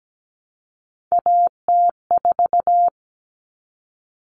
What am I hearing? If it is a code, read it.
Morse code, heard as a keyed tone, about 17 wpm, 700 Hz. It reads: AT4
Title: Common Cut Numbers - 3-Digit 17wpm